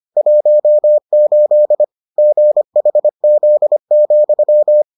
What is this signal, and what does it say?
Morse code, heard as a keyed tone, about 25 wpm, 600 Hz. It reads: 18 GHZ,